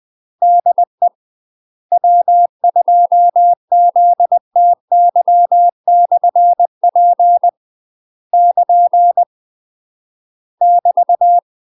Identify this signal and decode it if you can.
Morse code, heard as a keyed tone, about 20 wpm, 700 Hz. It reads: DE W2ZTY/P <KN> <BT>
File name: UK MASTER QSOs TRAINING / 20WPM / UK MASTER QSOs 0041